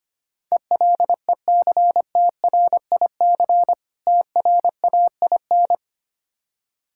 Morse code, heard as a keyed tone, about 25 wpm, 700 Hz. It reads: ELECTRIC TRAIN
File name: practice / 25wpm / Difficult Words with Hints - 25wpm